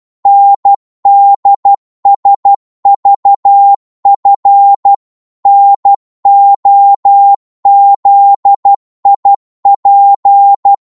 Morse code, heard as a keyed tone, about 12 wpm, 800 Hz. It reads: NDSVF NOZIP
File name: cwmp3 / ltr-12wpm